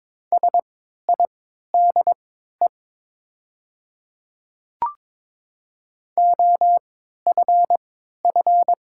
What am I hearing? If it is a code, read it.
Morse code, heard as a keyed tone, about 22 wpm, spaced as for 13 wpm, 700 Hz. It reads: SIDE OFF
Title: 3x Character Spacing - Top 500 Words - 22wpm